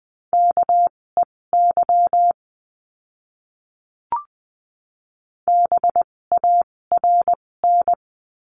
Morse code, heard as a keyed tone, about 20 wpm, spaced as for 16 wpm, 700 Hz. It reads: KEY BARN
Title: 256 Lesson 42a - Words 12-20wpm